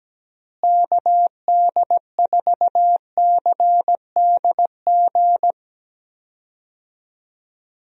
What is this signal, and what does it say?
Morse code, heard as a keyed tone, about 17 wpm, 700 Hz. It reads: KD4CDG
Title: USA Call Signs - 6 Characters - Repeated 2x - 17wpm